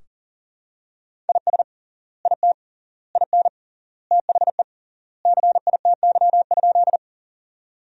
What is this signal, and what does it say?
Morse code, heard as a keyed tone, about 40 wpm, 700 Hz. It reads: IS IT IN THE CITY?
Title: 3x Word-Spacing - Sentences from Top 300 Words - 40wpm